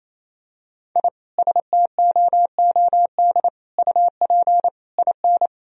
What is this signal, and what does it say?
Morse code, heard as a keyed tone, about 28 wpm, 700 Hz. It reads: I STOOD UP IN
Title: Sentences from Top 1000 Words - Review Part 2 - 28wpm